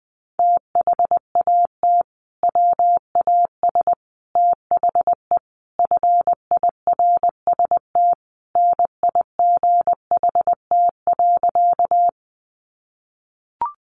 Morse code, heard as a keyed tone, about 20 wpm, 700 Hz. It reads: THAT WAS THE FIRST NIGHT.
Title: Sentences from Top 300 Words 20wpm